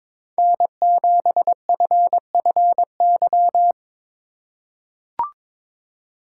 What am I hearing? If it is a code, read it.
Morse code, heard as a keyed tone, about 22 wpm, 700 Hz. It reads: N7FFY
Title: USA Call Signs - 5 Characters - 4x Character Spacing - 22wpm